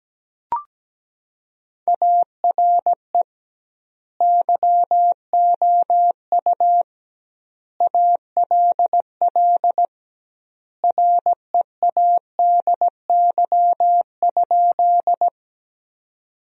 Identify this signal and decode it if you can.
Morse code, heard as a keyed tone, about 17 wpm, 700 Hz. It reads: ARE YOU ALL READY?
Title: 2x Word-Spacing - Sentences from Top 400 Words - 17wpm